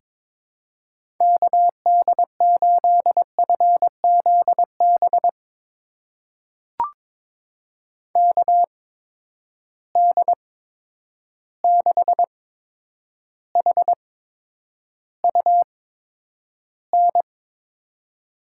Morse code, heard as a keyed tone, about 23 wpm, spaced as for 10 wpm, 700 Hz. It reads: KD8FZB KD6HUN
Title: USA Call Signs - 6 Characters - 8x Character Spacing - 22wpm